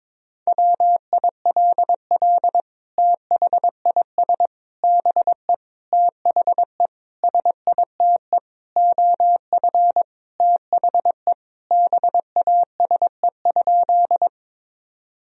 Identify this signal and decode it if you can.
Morse code, heard as a keyed tone, about 22 wpm, 700 Hz. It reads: WILL THIS BE THE SITE OF THE BASE?
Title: Sentences from Top 1000 Words - Review Part 1 - 22wpm